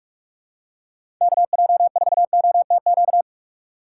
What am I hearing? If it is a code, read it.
Morse code, heard as a keyed tone, about 43 wpm, 700 Hz. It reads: KJ4OTQ
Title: USA Call Signs - 6 Characters - 8x Character Spacing - 45wpm